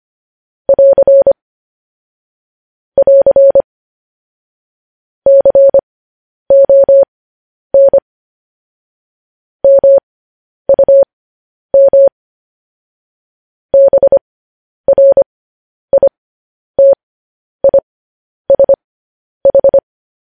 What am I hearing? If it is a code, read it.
Morse code, heard as a keyed tone, about 25 wpm, spaced as for 10 wpm, 550 Hz. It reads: <AR> <AR> CON MUM BRITISH
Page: news headlines in morse code at 10 wpm